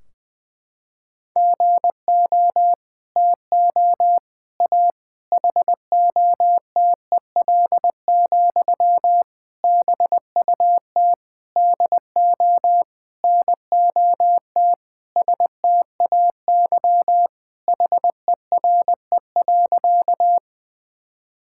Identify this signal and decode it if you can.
Morse code, heard as a keyed tone, about 20 wpm, 700 Hz. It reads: GO TO A HOTEL, BUT DO NOT STAY HERE.
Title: Sentences from Top 1600 Words 20wpm